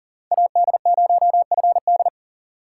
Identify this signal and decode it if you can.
Morse code, heard as a keyed tone, about 40 wpm, 700 Hz. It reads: AD0FD
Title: USA Call Signs - 5 Characters - 4x Character Spacing - 40wpm